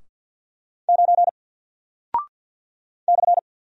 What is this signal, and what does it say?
Morse code, heard as a keyed tone, about 48 wpm, 700 Hz. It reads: 9 /
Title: Single Character (Letter:Number:Punctuation:Prosign) Alt 50wpm